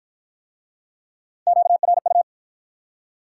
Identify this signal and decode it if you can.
Morse code, heard as a keyed tone, about 52 wpm, 700 Hz. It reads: QRU